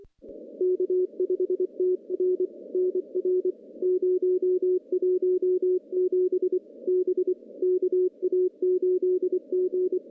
Morse code, heard as a keyed tone, about 24 wpm, 370 Hz. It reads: K5TR NR 017 B KA8G
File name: contests / CW-SS-k5tr- -1500